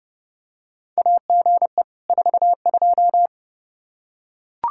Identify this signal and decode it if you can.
Morse code, heard as a keyed tone, about 31 wpm, 700 Hz. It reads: AGE 42
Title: Practice QSO - Exchange 3 - More Info - 30wpm